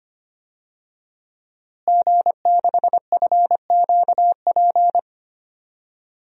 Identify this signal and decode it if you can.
Morse code, heard as a keyed tone, about 25 wpm, 700 Hz. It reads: G6FQP
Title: DXCC-100 Call Signs - 5 Characters - Repeated 3x - 25wpm